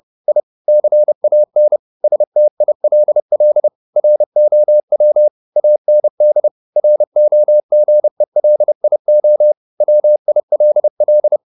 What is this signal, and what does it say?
Morse code, heard as a keyed tone, about 30 wpm, 600 Hz. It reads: I CAN STILL ROW AND ROGELIO WILL
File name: the old man and the sea / the old man and the sea 1 30wpm